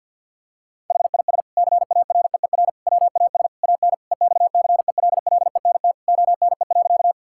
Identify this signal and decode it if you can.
Morse code, heard as a keyed tone, about 47 wpm, 700 Hz. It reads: HIS CAREER WAS AN EXCELLENT ONE.
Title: Sentences from Top 1000 Words - Review Part 6 - 50wpm